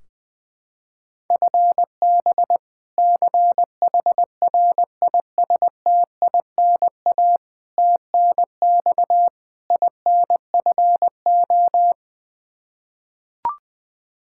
Morse code, heard as a keyed tone, about 21 wpm, 700 Hz. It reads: FB CHRISTINA TNX INFO
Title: Practice QSO - Exchange 3 - More Info - 20wpm